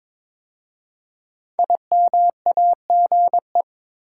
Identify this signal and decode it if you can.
Morse code, heard as a keyed tone, about 22 wpm, 700 Hz. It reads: IMAGE
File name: 5 Letter Words - First 3 Letters Given - 22wpm